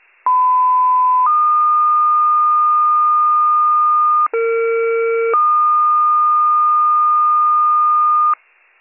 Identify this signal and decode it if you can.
Two-tone page, heard as a one-second tone, then a three-second tone, about 1010 then 1220 Hz, about 460 then 1220 Hz. It is heard from 0.3 to 8.3 s.